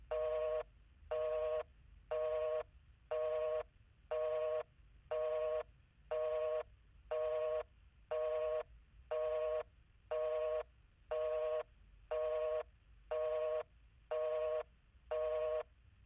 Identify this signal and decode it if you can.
Busy tone (North America), 480 + 620 Hz, with noise.
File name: domestic sound effects / phone-busy-1